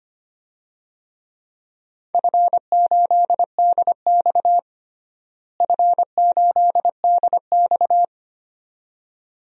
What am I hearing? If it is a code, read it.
Morse code, heard as a keyed tone, about 25 wpm, 700 Hz. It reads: F8DX F8DX
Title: DXCC-100 Call Signs - 4 Characters - Repeated 2x - 25wpm